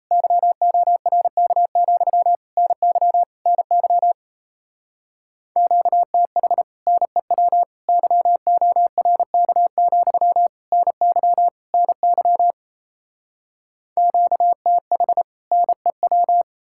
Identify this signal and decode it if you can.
Morse code, heard as a keyed tone, about 37 wpm, 700 Hz. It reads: YORK, NY NY QTH NEW YORK, NY NY QTH NEW